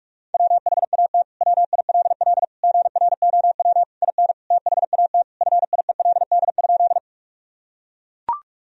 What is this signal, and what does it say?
Morse code, heard as a keyed tone, about 41 wpm, 700 Hz. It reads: WHAT WILL GROW IN THAT FIELD?
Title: Sentences from Top 1000 Words - Review Part 2 - 45wpm